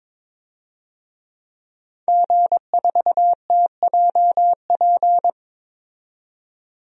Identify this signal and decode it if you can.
Morse code, heard as a keyed tone, about 22 wpm, 700 Hz. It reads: G4TJP
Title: DXCC-100 Call Signs - 5 Characters - 22wpm